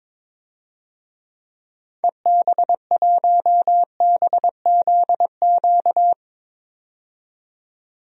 Morse code, heard as a keyed tone, about 22 wpm, 700 Hz. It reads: EB1BZQ
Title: DXCC-100 Call Signs - 6 Characters - 22wpm